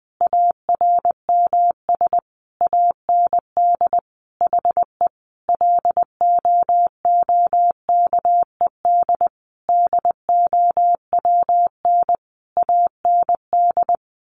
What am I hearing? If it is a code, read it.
Morse code, heard as a keyed tone, about 20 wpm, 700 Hz. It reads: ARMS AND HE LOOKED DOWN AND
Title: A Sound Of Thunder 20wpm